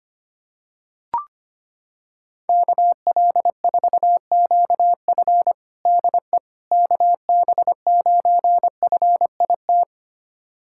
Morse code, heard as a keyed tone, about 25 wpm, 700 Hz. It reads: KL4QF DE KB9FIT
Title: USA Callsign DE Callsign 25wpm